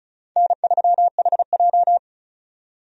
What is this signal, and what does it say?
Morse code, heard as a keyed tone, about 34 wpm, 700 Hz. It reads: N3HJ
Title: USA Call Signs - 4 Characters - 35wpm